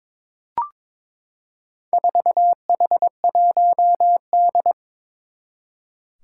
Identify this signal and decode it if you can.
Morse code, heard as a keyed tone, about 22 wpm, 700 Hz. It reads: 4H1D
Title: Four Challenging Letter-Number Combinations 22wpm